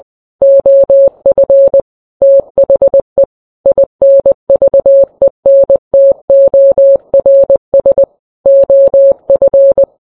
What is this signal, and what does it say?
Morse code, heard as a keyed tone, about 20 wpm, 550 Hz. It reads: OF THE INVENTORS OF